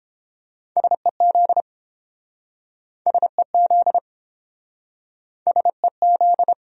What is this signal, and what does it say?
Morse code, heard as a keyed tone, about 32 wpm, 700 Hz. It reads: SEZ SEZ SEZ